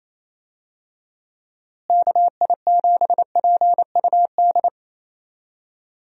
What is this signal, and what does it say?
Morse code, heard as a keyed tone, about 27 wpm, 700 Hz. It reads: KI7PUD
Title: USA Call Signs - 6 Characters - Repeated 2x - 28wpm